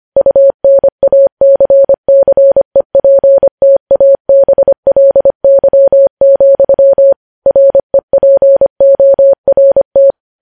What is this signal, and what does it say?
Morse code, heard as a keyed tone, about 25 wpm, 550 Hz. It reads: UNACCEPTABLY, REPORT
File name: podcast / News headlines-1x-25x25